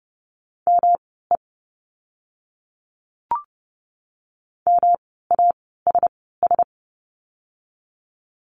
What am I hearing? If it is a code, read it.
Morse code, heard as a keyed tone, about 30 wpm, spaced as for 17 wpm, 700 Hz. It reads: ME MASS
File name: course / 30:10wpm / 108 Lesson 18f - Letter m - Words 30wpm